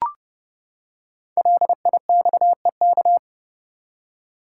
Morse code, heard as a keyed tone, about 30 wpm, 700 Hz. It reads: LIXEK
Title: Code Groups - 5 Letter:Numbers - 30wpm